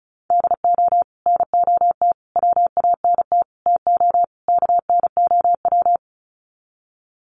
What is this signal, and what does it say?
Morse code, heard as a keyed tone, about 34 wpm, 700 Hz. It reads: DO NOT WANT TO KNOW
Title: Sets of 5 Words Encore 35wpm